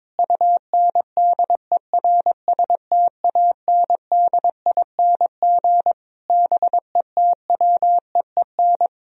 Morse code, heard as a keyed tone, about 22 wpm, 700 Hz. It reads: UNDERSTANDING BETWEEN
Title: Sentences from Top 1000 Words - Review Part 3 - 22wpm